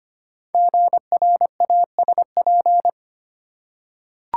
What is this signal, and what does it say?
Morse code, heard as a keyed tone, about 25 wpm, 700 Hz. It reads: GRASP
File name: 5 Letter Words - First 3 Letters Given - 25wpm